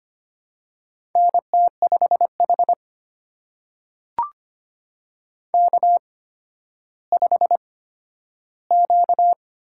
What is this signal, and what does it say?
Morse code, heard as a keyed tone, about 25 wpm, spaced as for 12 wpm, 700 Hz. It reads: NT5H K5Q
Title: USA Call Signs - 8x Character Spacing - 25wpm